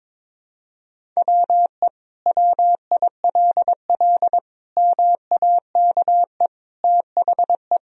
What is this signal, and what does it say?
Morse code, heard as a keyed tone, about 22 wpm, 700 Hz. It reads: WE WILL MAKE THE
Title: Sentences from Top 1000 Words - Review Part 1 - 22wpm